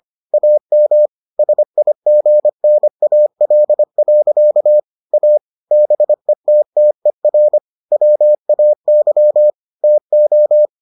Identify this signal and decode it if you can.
Morse code, heard as a keyed tone, about 25 wpm, 600 Hz. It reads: AM SIGNAL. A BETTER WAY TO